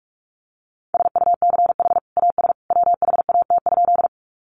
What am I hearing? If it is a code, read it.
Morse code, heard as a keyed tone, about 44 wpm, 700 Hz. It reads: SUCH AS WHAT?
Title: Sentences from Top 200 Words 45wpm